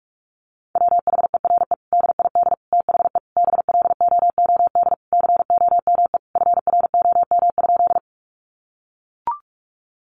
Morse code, heard as a keyed tone, about 44 wpm, 700 Hz. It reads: WHERE DID THE BLOOD COME FROM?
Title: Sentences from Top 1100 Words 45wpm